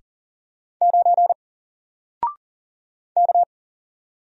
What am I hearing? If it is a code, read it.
Morse code, heard as a keyed tone, about 40 wpm, 700 Hz. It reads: 9 K
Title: Single Letter-Number 40wpm